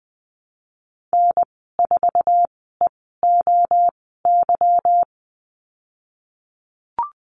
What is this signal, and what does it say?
Morse code, heard as a keyed tone, about 20 wpm, spaced as for 15 wpm, 700 Hz. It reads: N4EOY
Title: 147 Lesson 24g - Letter y - Call Signs 20wpm